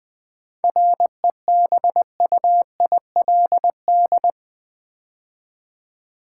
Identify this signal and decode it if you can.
Morse code, heard as a keyed tone, about 20 wpm, 700 Hz. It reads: REBUILD
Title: 7 Letter Words - First 3 Letters Given - 20wpm